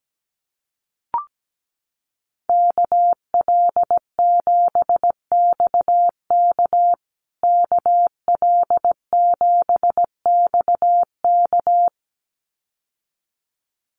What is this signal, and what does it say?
Morse code, heard as a keyed tone, about 17 wpm, 700 Hz. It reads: KL7XK KL7XK
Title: USA Call Signs Repeated 2x 17wpm